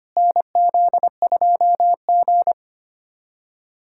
Morse code, heard as a keyed tone, about 25 wpm, 700 Hz. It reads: NZ2G
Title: USA Callsign DE Callsign - 4 Characters - 25wpm